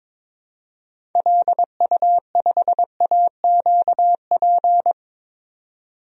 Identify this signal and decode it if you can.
Morse code, heard as a keyed tone, about 22 wpm, 700 Hz. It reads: LU5AQP
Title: DXCC-100 Call Signs - 6 Characters - 22wpm